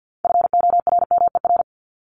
Morse code, heard as a keyed tone, about 50 wpm, 700 Hz. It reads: FORMER
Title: LY Suffix - Top 100 Words with Base Words 50wpm